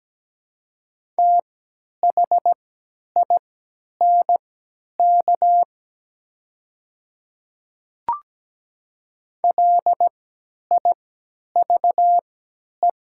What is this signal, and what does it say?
Morse code, heard as a keyed tone, about 17 wpm, spaced as for 10 wpm, 700 Hz. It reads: THINK LIVE
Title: 3x Character Spacing - Top 200 Words - 17wpm